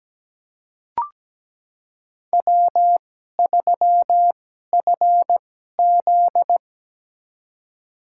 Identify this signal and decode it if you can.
Morse code, heard as a keyed tone, about 17 wpm, spaced as for 12 wpm, 700 Hz. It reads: W3FZ
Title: USA Call Signs - 4 Characters - 2x Character Spacing - 17wpm